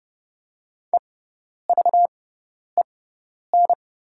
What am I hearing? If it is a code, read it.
Morse code, heard as a keyed tone, about 30 wpm, spaced as for 10 wpm, 690 Hz. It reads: EVEN
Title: 6x Character Spacing - Top 500 Words - 30wpm